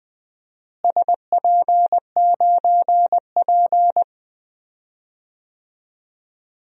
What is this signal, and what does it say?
Morse code, heard as a keyed tone, about 21 wpm, 700 Hz. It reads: SP9P
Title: DXCC-100 Call Signs - 4 Characters - Repeated 2x - 20wpm